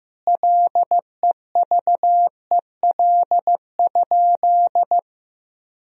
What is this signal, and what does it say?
Morse code, heard as a keyed tone, about 16 wpm, 700 Hz. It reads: LEVEL?
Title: Sentences from Top 1000 Words - Review Part 4 - 15wpm